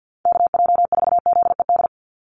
Morse code, heard as a keyed tone, about 50 wpm, 700 Hz. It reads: KJ4ZED
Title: USA Call Signs 50wpm